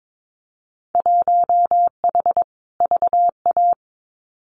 Morse code, heard as a keyed tone, about 22 wpm, 700 Hz. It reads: 1H VA
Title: Winter Field Day Exchanges 22wpm